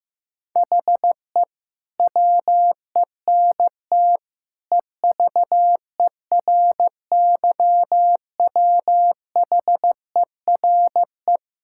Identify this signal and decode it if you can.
Morse code, heard as a keyed tone, about 14 wpm, 700 Hz. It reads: HE WENT EVERYWHERE
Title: Sentences from Top 1600 Words 15wpm